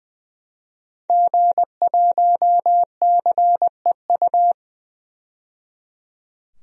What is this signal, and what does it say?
Morse code, heard as a keyed tone, about 20 wpm, 700 Hz. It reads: G1CEU
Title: DXCC-100 Call Signs - 5 Characters - 20wpm